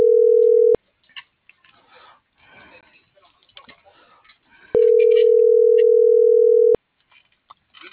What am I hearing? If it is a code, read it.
Ringback tone (North America), 440 + 480 Hz.